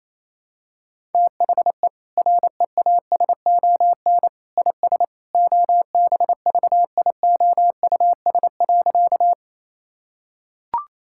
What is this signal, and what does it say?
Morse code, heard as a keyed tone, about 28 wpm, 700 Hz. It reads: THE REASON IS OBVIOUS.
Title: Sentences from Top 1600 Words 28wpm